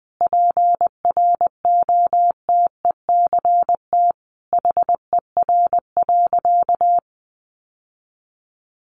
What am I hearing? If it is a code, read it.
Morse code, heard as a keyed tone, about 20 wpm, 700 Hz. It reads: PROTECT HER.
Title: Sentences from Top 900 Words 20wpm